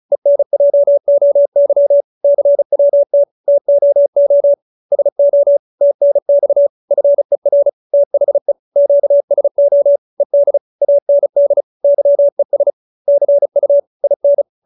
Morse code, heard as a keyed tone, about 34 wpm, 570 Hz. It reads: ENJOY CWT TOO SO TNX FER THE QSO ED AND YES CU IN